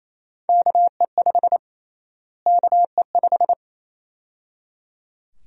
Morse code, heard as a keyed tone, about 28 wpm, 700 Hz. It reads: KE5 KE5
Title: USA Call Sign Prefixes - Repeated 2x - 28wpm